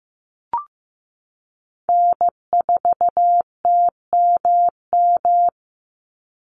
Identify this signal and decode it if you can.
Morse code, heard as a keyed tone, about 15 wpm, 700 Hz. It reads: N4TMM
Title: Call Signs Repeated 3x - 15wpm